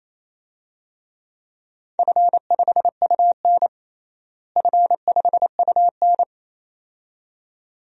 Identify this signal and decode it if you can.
Morse code, heard as a keyed tone, about 28 wpm, 700 Hz. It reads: F5UN F5UN
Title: DXCC-100 Call Signs - 4 Characters - Repeated 2x - 28wpm